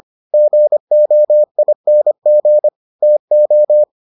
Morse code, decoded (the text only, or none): GOING TO